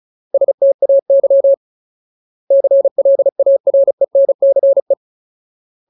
Morse code, heard as a keyed tone, about 34 wpm, 550 Hz. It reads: STAY CLARENCE
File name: Ultimate Word List / 35WPM Ultimate